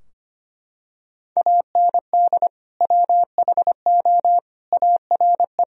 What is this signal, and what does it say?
Morse code, heard as a keyed tone, about 25 wpm, 700 Hz. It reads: AND WHO ARE
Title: Sentences from Top 1000 Words - Review Part 2 - 25wpm